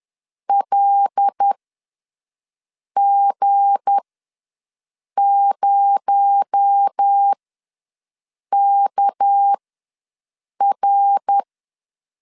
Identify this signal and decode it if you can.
Morse code, heard as a keyed tone, about 10 wpm, spaced as for 5 wpm, 800 Hz. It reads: LG0KR
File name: LEZIONI / 9 2867 ws 10wpm 4215